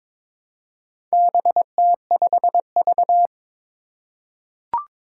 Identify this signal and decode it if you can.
Morse code, heard as a keyed tone, about 22 wpm, 700 Hz. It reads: BT5V
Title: Four Challenging Letter-Number Combinations 22wpm